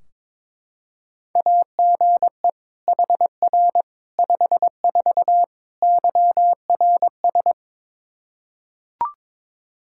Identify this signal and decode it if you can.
Morse code, heard as a keyed tone, about 22 wpm, 700 Hz. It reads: AGE HR 54 YRS
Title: Practice QSO - Exchange 3 - More Info - 22wpm